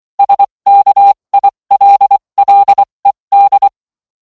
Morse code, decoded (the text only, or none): SKILLED